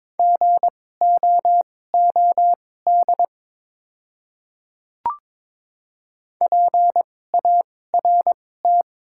Morse code, heard as a keyed tone, about 23 wpm, spaced as for 16 wpm, 700 Hz. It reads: GOOD PART
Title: 2x Character Spacing - Top 300 Words - 22wpm